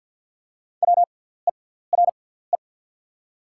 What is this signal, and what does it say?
Morse code, heard as a keyed tone, about 49 wpm, spaced as for 17 wpm, 690 Hz. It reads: WERE
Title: 6x Character Spacing - Top 200 Words - 50wpm